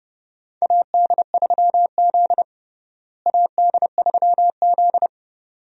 Morse code, heard as a keyed tone, about 32 wpm, 700 Hz. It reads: AD3Z AD3Z
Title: USA Call Signs - 4 Characters - Repeated 2x - 30wpm